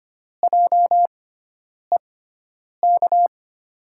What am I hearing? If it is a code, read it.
Morse code, heard as a keyed tone, about 25 wpm, spaced as for 9 wpm, 700 Hz. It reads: JEK
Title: USA Call Signs - 5 Characters - 6x Character Spacing - 25wpm